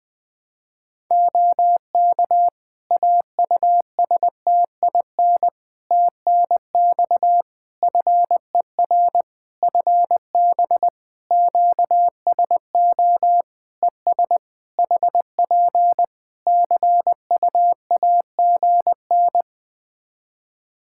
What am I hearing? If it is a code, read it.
Morse code, heard as a keyed tone, about 21 wpm, 700 Hz. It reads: OK AUSTIN TNX FER FB QSO ES HP CUAGN
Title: Practice QSO - Exchange 4 - Ending - 20wpm